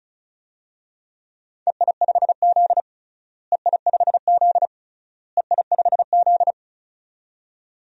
Morse code, heard as a keyed tone, about 34 wpm, 700 Hz. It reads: EI5Z EI5Z EI5Z
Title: DXCC-100 Call Signs - 4 Characters - Repeated 3x - 35wpm